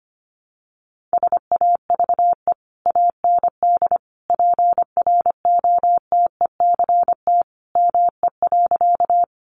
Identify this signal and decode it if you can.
Morse code, heard as a keyed tone, about 25 wpm, 700 Hz. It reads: SAVE AND PROTECT ME.
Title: Sentences from Top 1100 Words 25wpm